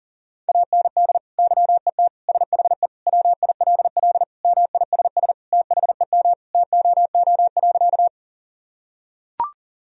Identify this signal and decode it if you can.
Morse code, heard as a keyed tone, about 37 wpm, 700 Hz. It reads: AND YET SHE WILL MISS THEM TOO.
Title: Sentences from Top 1000 Words - Review Part 3 - 40wpm